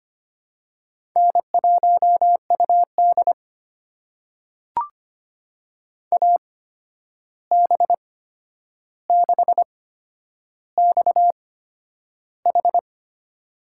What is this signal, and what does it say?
Morse code, heard as a keyed tone, about 25 wpm, spaced as for 10 wpm, 700 Hz. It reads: N1UD AB6XH
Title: USA Call Signs - 8x Character Spacing - 25wpm